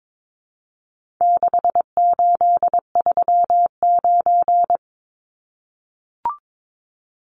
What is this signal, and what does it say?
Morse code, heard as a keyed tone, about 22 wpm, 700 Hz. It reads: 6839
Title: Numbers - 4-Digit 22wpm